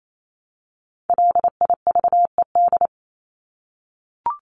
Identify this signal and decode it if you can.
Morse code, heard as a keyed tone, about 28 wpm, 700 Hz. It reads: LIVED
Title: ED Suffix - Top 100 Words with Base Words 28wpm